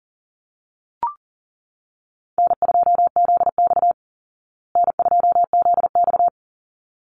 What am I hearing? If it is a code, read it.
Morse code, heard as a keyed tone, about 40 wpm, 700 Hz. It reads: N2ZX N2ZX
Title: Easy Call Signs Repeated 2x - 40wpm